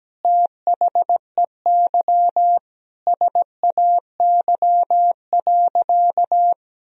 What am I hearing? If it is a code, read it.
Morse code, heard as a keyed tone, about 17 wpm, 700 Hz. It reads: THEY SAY.
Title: Sentences from Top 1000 Words - Review Part 2 - 17wpm